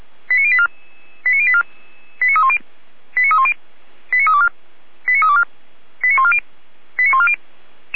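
Five-tone selective call (ZVEI), heard as quick runs of five tones, tones of about 70 ms. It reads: 89084 89084 89310 89310 89324 89324 89140 89140
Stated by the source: Five-tone ZVEI selective call 89250, received by SDR, from the Feuerwehr Orlamünde (fire brigade) audio archive.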